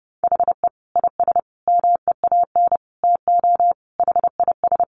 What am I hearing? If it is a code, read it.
Morse code, heard as a keyed tone, about 28 wpm, 700 Hz. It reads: HE IS MEAN TO HIS
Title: Additional Easy Phrases 30wpm